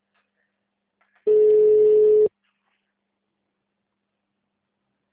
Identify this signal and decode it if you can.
Ringback tone (Europe, 425 Hz).